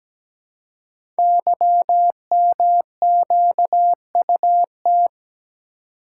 Morse code, heard as a keyed tone, about 17 wpm, 700 Hz. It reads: YMQUT
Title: Code Groups - 5 Letters - 17wpm